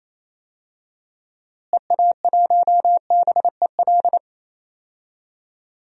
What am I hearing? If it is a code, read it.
Morse code, heard as a keyed tone, about 28 wpm, 700 Hz. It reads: EA1BEL
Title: DXCC-100 Call Signs - 6 Characters - Repeated 2x - 28wpm